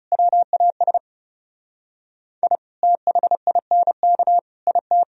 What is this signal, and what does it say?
Morse code, heard as a keyed tone, about 35 wpm, 700 Hz. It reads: WAS I THINK IT
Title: Speed-Racing - Sets of 4 Words 30wpm